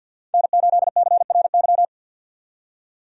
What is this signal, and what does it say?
Morse code, heard as a keyed tone, about 50 wpm, 700 Hz. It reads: N8CRY
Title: USA Call Signs - 5 Characters - 50wpm